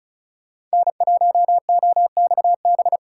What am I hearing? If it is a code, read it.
Morse code, heard as a keyed tone, about 36 wpm, 700 Hz. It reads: N1OXB
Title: USA Call Signs - 5 Characters - 2x Character Spacing - 35wpm